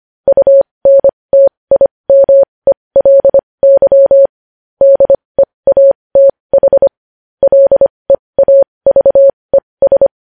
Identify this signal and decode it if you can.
Morse code, heard as a keyed tone, about 25 wpm, spaced as for 20 wpm, 550 Hz. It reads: UNTIMELY DEATH LEAVES
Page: news headlines in morse code at 20 wpm